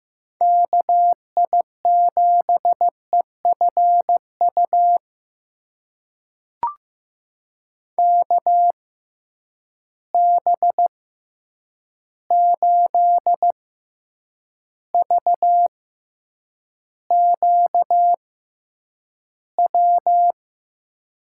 Morse code, heard as a keyed tone, about 15 wpm, spaced as for 8 wpm, 700 Hz. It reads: KI7EFU KB8VQW
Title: USA Call Signs - 6 Characters - 6x Character Spacing - 15wpm